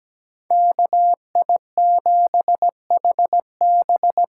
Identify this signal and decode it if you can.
Morse code, heard as a keyed tone, about 17 wpm, 700 Hz. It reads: KI7HB
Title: USA Call Signs - 6 Characters - 6x Character Spacing - 17wpm